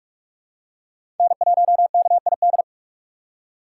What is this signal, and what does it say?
Morse code, heard as a keyed tone, about 47 wpm, 700 Hz. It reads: N1KID
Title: USA Call Signs - 5 Characters - 2x Character Spacing - 45wpm